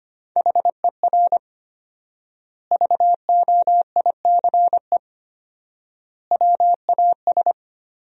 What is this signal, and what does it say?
Morse code, heard as a keyed tone, about 25 wpm, 700 Hz. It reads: HER VOICE WAS
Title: 4x Word-Spacing - Sentences from Top 500 Words - 25wpm